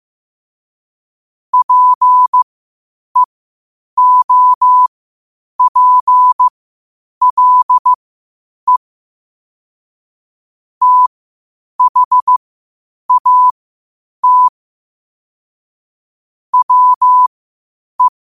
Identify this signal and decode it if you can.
Morse code, heard as a keyed tone, about 15 wpm, spaced as for 8 wpm, 1000 Hz. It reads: PEOPLE THAT WE
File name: qotdfiles / 8wpm